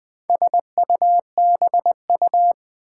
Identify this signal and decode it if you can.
Morse code, heard as a keyed tone, about 20 wpm, 700 Hz. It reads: SUBU
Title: Build A Word - 4 Letters At A Time - 8-Letter Words - 20wpm